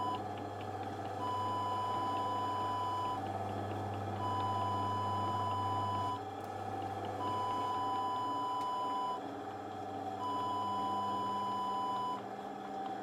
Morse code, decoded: none